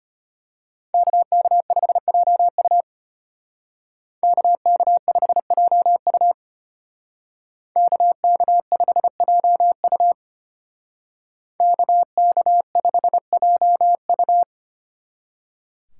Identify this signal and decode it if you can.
Morse code, heard as a keyed tone, about 33 wpm, 700 Hz. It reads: KK5JU KK5JU KK5JU KK5JU